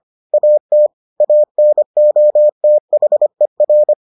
Morse code, heard as a keyed tone, about 25 wpm, 600 Hz. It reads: AT ANOTHER